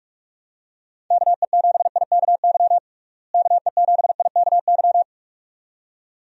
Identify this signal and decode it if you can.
Morse code, heard as a keyed tone, about 41 wpm, 700 Hz. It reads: KE7IKY KE7IKY
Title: USA Call Signs - 6 Characters - Repeated 2x - 45wpm